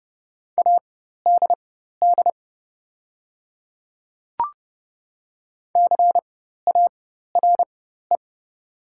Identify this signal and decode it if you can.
Morse code, heard as a keyed tone, about 30 wpm, spaced as for 14 wpm, 700 Hz. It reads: ADD CARE
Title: 4x Character Spacing - Top 300 Words - 30wpm